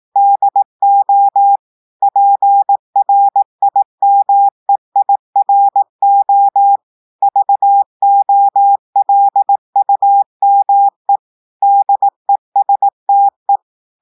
Morse code, heard as a keyed tone, about 18 wpm, 800 Hz. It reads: DO PRIMEIRO VOLUME DESTE
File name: audios / portugais / portugais texte en clair 1 vitesse 18 mots minutes